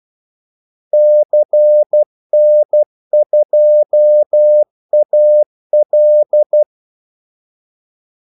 Morse code, heard as a keyed tone, about 12 wpm, 600 Hz. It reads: CN2AL
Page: 2 ekstra ordmellemrum mellem kaldesignalerne